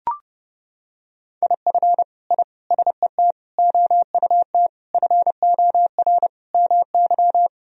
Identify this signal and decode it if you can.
Morse code, heard as a keyed tone, about 31 wpm, 700 Hz. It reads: IF I SET OUT FOR MY
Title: Sentences from Top 1000 Words - Review Part 2 - 30wpm